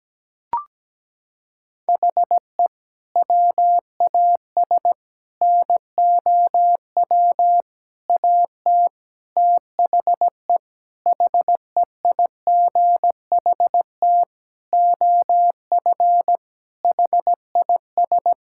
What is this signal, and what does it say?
Morse code, heard as a keyed tone, about 17 wpm, 700 Hz. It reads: HE WAS NOW AT THE HEIGHT OF HIS
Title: Sentences from Top 1600 Words 17wpm